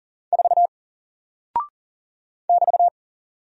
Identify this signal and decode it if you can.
Morse code, heard as a keyed tone, about 39 wpm, 700 Hz. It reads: 4 <BT>
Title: Single Character (Letter:Number:Punctuation:Prosign) 40wpm